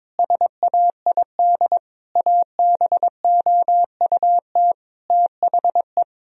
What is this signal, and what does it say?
Morse code, heard as a keyed tone, about 21 wpm, 700 Hz. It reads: SAID ABOUT THE
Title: Sentences from Top 1000 Words - Review Part 2 - 22wpm